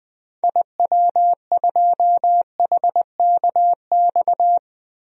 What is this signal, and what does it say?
Morse code, heard as a keyed tone, about 20 wpm, 700 Hz. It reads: IW2HKX
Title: DXCC-100 Call Signs - 6 Characters - Repeated 3x - 20wpm